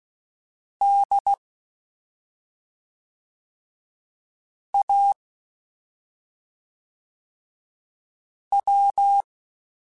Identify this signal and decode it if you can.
Morse code, heard as a keyed tone, about 16 wpm, 800 Hz. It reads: D A W
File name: Special INGs / Special INGs-mp3s / SlowToFaster ing @5